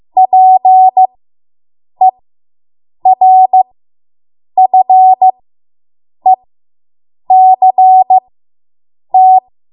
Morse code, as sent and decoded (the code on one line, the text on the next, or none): .--. .-. ..-. -.-. -
PERFECT